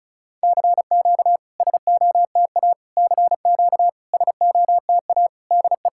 Morse code, heard as a keyed tone, about 36 wpm, 700 Hz. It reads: CQ SOTA CQ SOTA DE
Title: SOTA QSO Practice 35wpm